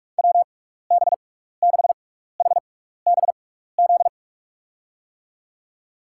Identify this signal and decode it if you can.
Morse code, heard as a keyed tone, about 46 wpm, spaced as for 16 wpm, 700 Hz. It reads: WB6HBZ
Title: USA Call Signs - 6 Characters - 6x Character Spacing - 45wpm